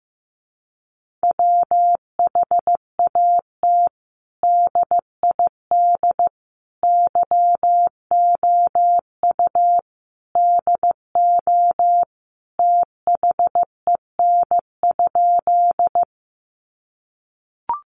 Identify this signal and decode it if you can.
Morse code, heard as a keyed tone, about 15 wpm, 700 Hz. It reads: WHAT DID YOU DO THEN?
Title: Sentences from Top 100 Words 15wpm